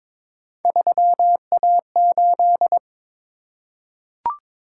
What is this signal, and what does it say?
Morse code, heard as a keyed tone, about 22 wpm, 700 Hz. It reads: 3A8
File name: practice / 22wpm / Three Letter-Number Combinations - 22wpm